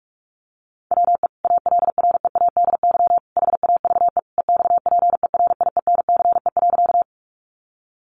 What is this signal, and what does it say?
Morse code, heard as a keyed tone, about 44 wpm, 700 Hz. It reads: WE ALREADY HAVE EXPERIENCE.
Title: Sentences from Top 700 Words 45wpm